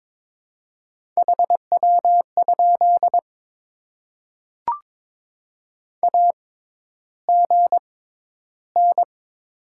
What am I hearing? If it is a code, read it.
Morse code, heard as a keyed tone, about 22 wpm, spaced as for 11 wpm, 700 Hz. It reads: HW? AGN
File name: Top 20 QSO Elements - 6x Character Spacing - 22wpm